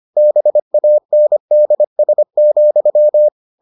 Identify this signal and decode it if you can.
Morse code, heard as a keyed tone, about 25 wpm, 600 Hz. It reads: BANDS,